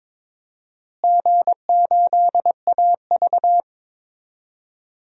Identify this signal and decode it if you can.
Morse code, heard as a keyed tone, about 22 wpm, 700 Hz. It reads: G8AV
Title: DXCC-100 Call Signs - 4 Characters - Repeated 2x - 22wpm